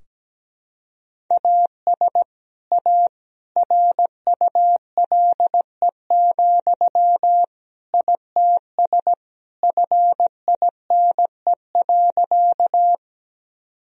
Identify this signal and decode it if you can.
Morse code, heard as a keyed tone, about 17 wpm, 700 Hz. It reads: AS A RULE, ITS FINE.
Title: Sentences from Top 1000 Words - Review Part 2 - 17wpm